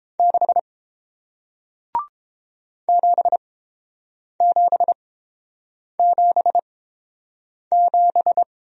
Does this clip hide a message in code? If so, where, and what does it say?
Yes: Morse code from 0.2 to 8.4 s, heard as a keyed tone, about 30 wpm, spaced as for 7 wpm, 700 Hz. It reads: 6 7777